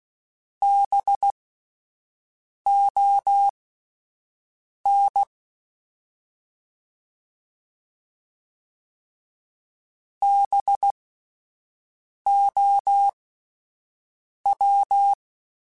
Morse code, heard as a keyed tone, about 16 wpm, spaced as for 5 wpm, 800 Hz. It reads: BON BOW
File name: Spaced Alphas / Spaced Alphas mp3s / Bs sp @10